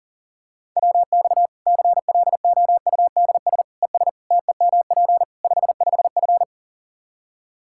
Morse code, heard as a keyed tone, about 40 wpm, 700 Hz. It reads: WX CLOUDS ES TEMP 55F
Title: K1USN SST QSO Practice 40wpm